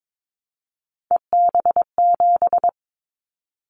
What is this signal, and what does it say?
Morse code, heard as a keyed tone, about 22 wpm, 700 Hz. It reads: EB7
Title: Callsign Prefixes for Top 25 DXCC - 22wpm